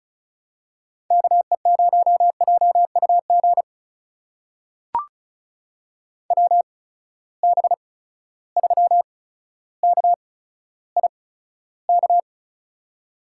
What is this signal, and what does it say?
Morse code, heard as a keyed tone, about 36 wpm, spaced as for 15 wpm, 700 Hz. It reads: KE0JUG WB3KIK
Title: USA Call Signs - 6 Characters - 8x Character Spacing - 35wpm